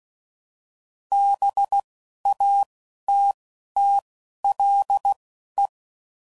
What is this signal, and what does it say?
Morse code, heard as a keyed tone, about 16 wpm, spaced as for 11 wpm, 800 Hz. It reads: BATTLE
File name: CW Code Groups @10WPM / fromSM's / Cities